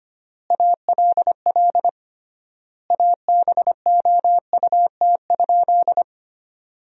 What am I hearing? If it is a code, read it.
Morse code, heard as a keyed tone, about 25 wpm, 700 Hz. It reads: ALL ABOUT?
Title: 3x Word-Spacing - Sentences from Top 100 Words - 25wpm